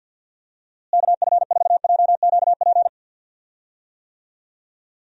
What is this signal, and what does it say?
Morse code, heard as a keyed tone, about 49 wpm, 700 Hz. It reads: KF4JQP
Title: USA Call Signs - 6 Characters - Repeated 2x - 50wpm